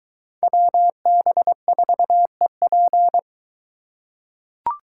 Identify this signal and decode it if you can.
Morse code, heard as a keyed tone, about 21 wpm, 700 Hz. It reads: WB4EP